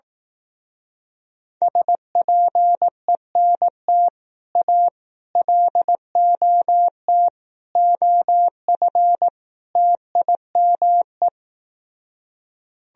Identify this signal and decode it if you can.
Morse code, heard as a keyed tone, about 18 wpm, 700 Hz. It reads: SPENT A LOT OF TIME
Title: Speed-Racing - Sets of 5 Words 15wpm